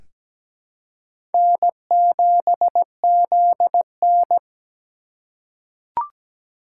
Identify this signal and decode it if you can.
Morse code, heard as a keyed tone, about 17 wpm, 700 Hz. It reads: N7ZN
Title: K1USN SST QSO Practice - 17wpm